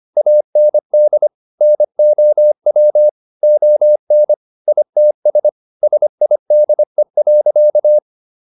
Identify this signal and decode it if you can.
Morse code, heard as a keyed tone, about 25 wpm, 600 Hz. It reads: AND NOW ON ITS SIDE.